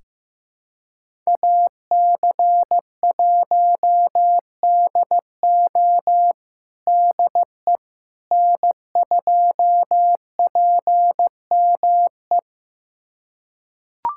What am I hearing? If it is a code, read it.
Morse code, heard as a keyed tone, about 15 wpm, 700 Hz. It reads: AC1DO DE N2PME
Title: USA Callsign DE Callsign - 5 Characters - 15wpm